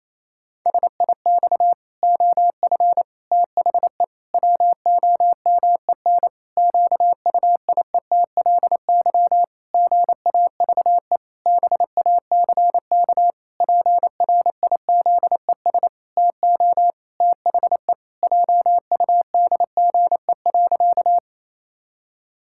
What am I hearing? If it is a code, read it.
Morse code, heard as a keyed tone, about 28 wpm, 700 Hz. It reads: SIX OF THE WOMEN QUIETLY GAVE BACK PRIZES TO THE JUDGE.
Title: Pangram Challenge v2 - 28wpm